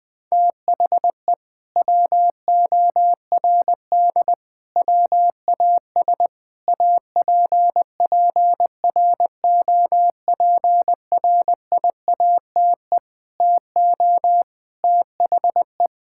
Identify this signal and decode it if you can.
Morse code, heard as a keyed tone, about 21 wpm, 700 Hz. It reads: THE WORD WAS APPROPRIATE TO THE
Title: Sentences from Top 1000 Words - Review Part 5 - 20wpm